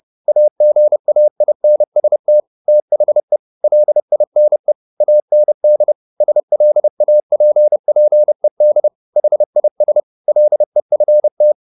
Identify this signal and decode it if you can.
Morse code, heard as a keyed tone, about 30 wpm, 600 Hz. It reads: AGAINST THE LINE AND SLAPPED HIS LEFT
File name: the old man and the sea / the old man and the sea 1 30wpm